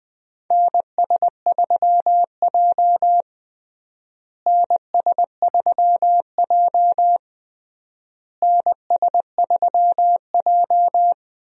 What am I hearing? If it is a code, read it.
Morse code, heard as a keyed tone, about 20 wpm, 700 Hz. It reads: NS3J NS3J NS3J
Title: USA Call Signs - 4 Characters - Repeated 3x - 20wpm